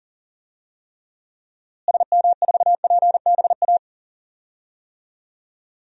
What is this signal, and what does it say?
Morse code, heard as a keyed tone, about 40 wpm, 700 Hz. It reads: SM4PBA
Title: DXCC-100 Call Signs - 6 Characters - Repeated 3x - 40wpm